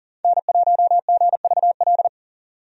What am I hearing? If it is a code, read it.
Morse code, heard as a keyed tone, about 40 wpm, 700 Hz. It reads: N1GVL
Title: USA Call Signs - 5 Characters - 40wpm